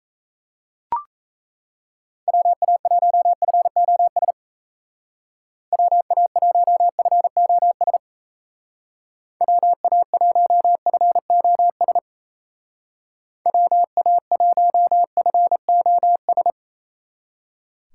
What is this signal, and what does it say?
Morse code, heard as a keyed tone, about 34 wpm, 700 Hz. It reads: WA1FOS WA1FOS WA1FOS WA1FOS
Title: Speed Racing - USA 2x3 Call Signs 28wpm